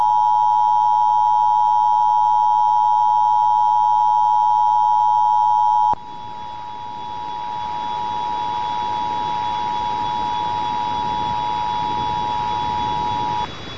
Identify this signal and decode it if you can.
Emergency Alert System (SAME) tones (the 853 and 960 Hz attention tone). It is heard from 0.5 to 13.0 s.